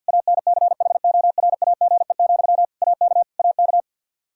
Morse code, heard as a keyed tone, about 52 wpm, 700 Hz. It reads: ANCHORAGE, AK AK
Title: Practice QSO - Exchange 1 - Beginning - 50wpm